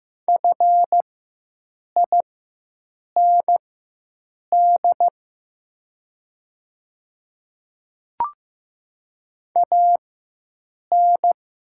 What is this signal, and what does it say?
Morse code, heard as a keyed tone, about 15 wpm, spaced as for 7 wpm, 700 Hz. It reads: FIND AN
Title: 4x Character Spacing - Top 100 Words - 15wpm